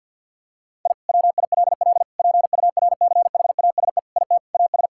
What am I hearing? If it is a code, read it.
Morse code, heard as a keyed tone, about 50 wpm, 700 Hz. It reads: I WILL PURCHASE IT AS
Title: Sentences from Top 1000 Words - Review Part 5 - 50wpm